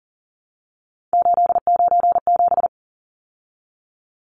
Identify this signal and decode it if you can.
Morse code, heard as a keyed tone, about 40 wpm, 700 Hz. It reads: 897
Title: Numbers - 3-Digit 40wpm